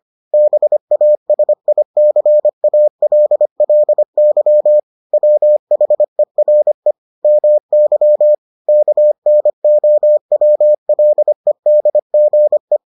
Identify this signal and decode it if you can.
Morse code, heard as a keyed tone, about 25 wpm, 600 Hz. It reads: BASICALLY WHERE MY KNOWLEDGE